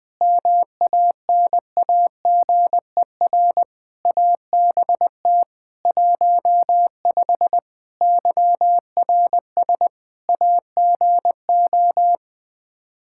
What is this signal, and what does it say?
Morse code, heard as a keyed tone, about 20 wpm, 700 Hz. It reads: MANAGER ABT 15 YRS AGO
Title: Practice QSO - Exchange 3 - More Info - 20wpm